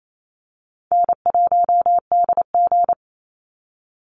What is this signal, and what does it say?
Morse code, heard as a keyed tone, about 28 wpm, 700 Hz. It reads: N1DG
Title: Easy Call Signs Repeated 3x - 28wpm